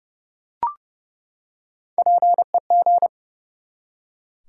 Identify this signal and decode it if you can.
Morse code, heard as a keyed tone, about 30 wpm, 700 Hz. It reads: PEG
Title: USA Common Names - Max 3 Letters - 30wpm